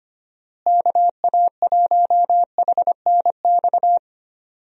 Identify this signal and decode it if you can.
Morse code, heard as a keyed tone, about 25 wpm, 700 Hz. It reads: KA1HNX
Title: USA Call Signs - 6 Characters - Repeated 2x - 25wpm